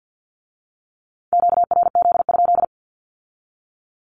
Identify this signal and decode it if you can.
Morse code, heard as a keyed tone, about 50 wpm, 700 Hz. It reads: QRZ?
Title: Q-Codes with Spoken Meaning - 50wpm